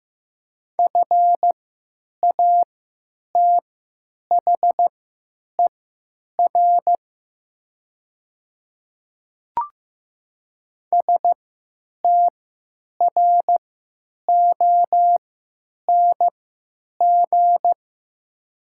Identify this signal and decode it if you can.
Morse code, heard as a keyed tone, about 15 wpm, spaced as for 8 wpm, 700 Hz. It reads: FATHER STRONG
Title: 3x Character Spacing - Top 500 Words - 15wpm